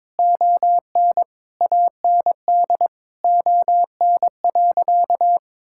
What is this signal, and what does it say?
Morse code, heard as a keyed tone, about 21 wpm, 700 Hz. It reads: ON AND ON.
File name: practice / 22wpm / Sentences with 8-Word Phrases - 22wpm